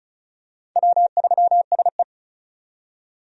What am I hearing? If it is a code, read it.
Morse code, heard as a keyed tone, about 34 wpm, 700 Hz. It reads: W3SE
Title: USA Call Signs - 4 Characters - 35wpm